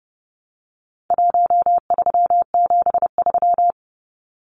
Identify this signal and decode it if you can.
Morse code, heard as a keyed tone, about 30 wpm, 700 Hz. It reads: 1373
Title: Numbers - Contest Style - 4-Digit 30wpm